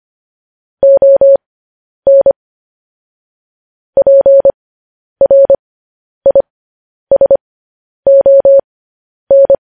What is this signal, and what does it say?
Morse code, heard as a keyed tone, about 25 wpm, spaced as for 10 wpm, 550 Hz. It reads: ON PRISON
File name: podcast / News headlines-1x-25x10